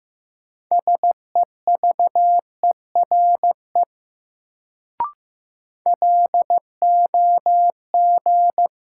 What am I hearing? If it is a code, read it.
Morse code, heard as a keyed tone, about 15 wpm, 700 Hz. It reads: SEVERE LOG
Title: Sentences from Top 1400 Words - New Word Drill Alt - 15wpm